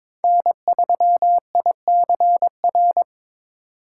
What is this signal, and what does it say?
Morse code, heard as a keyed tone, about 26 wpm, 700 Hz. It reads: N3ICR
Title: Speed Racing - USA 1x3 Call Signs 22wpm